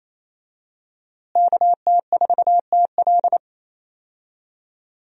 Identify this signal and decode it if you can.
Morse code, heard as a keyed tone, about 28 wpm, 700 Hz. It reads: KT4TL
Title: USA Call Signs - 5 Characters - Repeated 2x - 28wpm